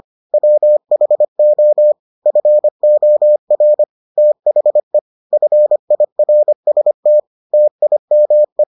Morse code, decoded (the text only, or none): WHO FOR THE FIRST TIME